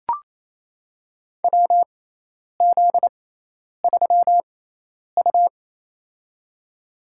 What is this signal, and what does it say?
Morse code, heard as a keyed tone, about 28 wpm, spaced as for 10 wpm, 700 Hz. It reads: WZ3U
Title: USA Call Signs - 4 Characters - 6x Character Spacing - 28wpm